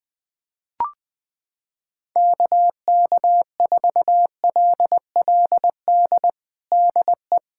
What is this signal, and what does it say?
Morse code, heard as a keyed tone, about 20 wpm, 700 Hz. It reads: KK4LLD DE
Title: USA Callsign DE Callsign 20wpm